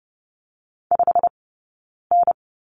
Morse code, heard as a keyed tone, about 30 wpm, 700 Hz. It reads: 5 N
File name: Lesson 19d - Letter w - Two Characters 30wpm